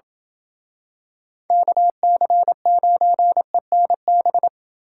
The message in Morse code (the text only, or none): KC9ENB